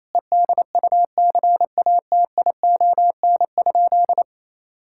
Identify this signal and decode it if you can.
Morse code, heard as a keyed tone, about 28 wpm, 700 Hz. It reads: EDUCATION?
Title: Sentences from Top 1000 Words - Review Part 6 - 28wpm